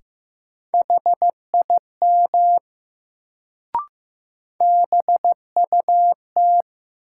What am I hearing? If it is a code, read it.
Morse code, heard as a keyed tone, about 15 wpm, 700 Hz. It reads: HIM BUT
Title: Sentences from Top 100 Words - New Word Drill - 15wpm